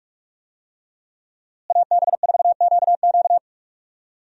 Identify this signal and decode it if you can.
Morse code, heard as a keyed tone, about 43 wpm, 700 Hz. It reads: AB4QQ
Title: USA Call Signs - 5 Characters - Repeated 2x - 45wpm